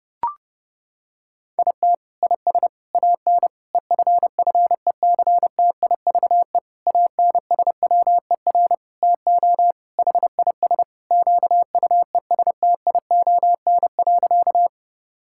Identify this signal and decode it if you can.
Morse code, heard as a keyed tone, about 30 wpm, 700 Hz. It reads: IT IS AN EFFECTIVE ANSWER TO HIS QUESTION.
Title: Sentences from Top 1000 Words - Review Part 6 - 30wpm